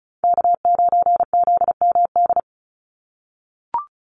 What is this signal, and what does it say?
Morse code, heard as a keyed tone, about 34 wpm, 700 Hz. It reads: K9ZMD
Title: NA SOTA Chaser Callsigns 35wpm